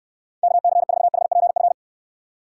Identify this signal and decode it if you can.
Morse code, heard as a keyed tone, about 69 wpm, 700 Hz. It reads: KC4HPF